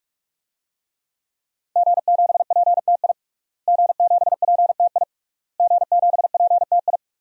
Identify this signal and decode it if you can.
Morse code, heard as a keyed tone, about 43 wpm, 700 Hz. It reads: G7PTI G7PTI G7PTI
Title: DXCC-100 Call Signs - 5 Characters - Repeated 3x - 45wpm